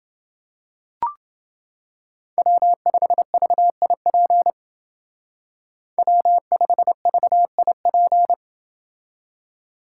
Morse code, heard as a keyed tone, about 28 wpm, 700 Hz. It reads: W5VIP W5VIP